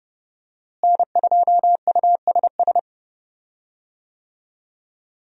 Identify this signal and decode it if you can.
Morse code, heard as a keyed tone, about 32 wpm, 700 Hz. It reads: N2USS
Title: USA Call Signs - 5 Characters - Repeated 2x - 30wpm